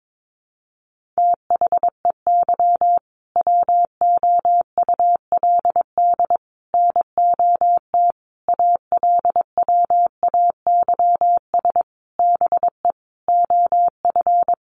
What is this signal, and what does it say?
Morse code, heard as a keyed tone, about 22 wpm, 700 Hz. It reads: THEY WOULD NOT ALWAYS BE OF
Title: Sentences from Top 1400 Words 22wpm